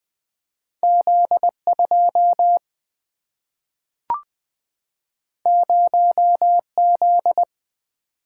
Morse code, heard as a keyed tone, about 20 wpm, 700 Hz. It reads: Z 2 0 Z
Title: Single-Letter Single-Number 20wpm